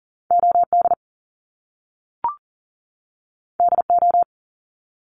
Morse code, heard as a keyed tone, about 41 wpm, 700 Hz. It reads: OD DO
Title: Top Two Letter Combinations in Words 40wpm